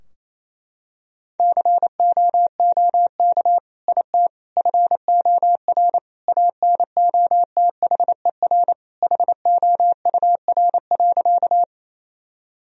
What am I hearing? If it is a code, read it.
Morse code, heard as a keyed tone, about 28 wpm, 700 Hz. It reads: COOK IT FOR ANOTHER HOUR.
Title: Sentences from Top 1000 Words - Review Part 4 - 28wpm